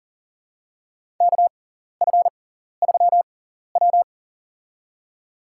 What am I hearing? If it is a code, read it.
Morse code, heard as a keyed tone, about 40 wpm, spaced as for 14 wpm, 700 Hz. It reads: KF3W
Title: USA Call Signs - 4 Characters - 6x Character Spacing - 40wpm